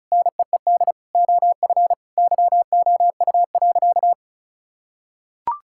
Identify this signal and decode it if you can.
Morse code, heard as a keyed tone, about 36 wpm, 700 Hz. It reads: NEED OF YOU.
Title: Sentences from Top 1000 Words - Review Part 5 - 35wpm